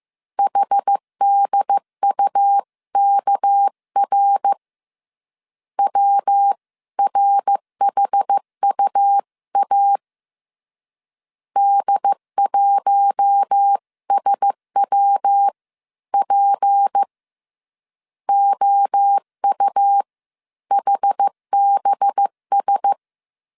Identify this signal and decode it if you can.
Morse code, heard as a keyed tone, about 15 wpm, spaced as for 13 wpm, 800 Hz. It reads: HDUKR WRHUA D1SWP OUHBS